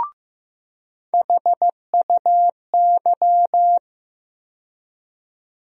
Morse code, heard as a keyed tone, about 15 wpm, 700 Hz. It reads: HUY
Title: USA Call Sign Postfixes - 3 Letters 15wpm